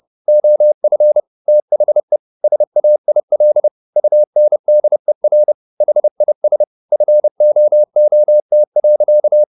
Morse code, heard as a keyed tone, about 30 wpm, 600 Hz. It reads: OF THE SAIL UNDER HIS FOOT.